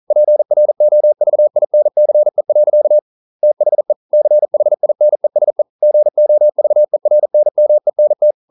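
Morse code, heard as a keyed tone, about 42 wpm, 600 Hz. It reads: PROVINCE. THE CHINESE GOVERNMENT
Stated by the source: Morse code practice at 41 wpm, from CW med Gnister.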